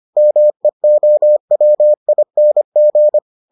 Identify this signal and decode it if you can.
Morse code, heard as a keyed tone, about 25 wpm, 600 Hz. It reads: MEOWING